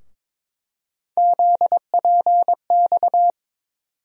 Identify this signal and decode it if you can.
Morse code, heard as a keyed tone, about 22 wpm, 700 Hz. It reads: ZPX
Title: USA Call Sign Postfixes - 3 Letters 22wpm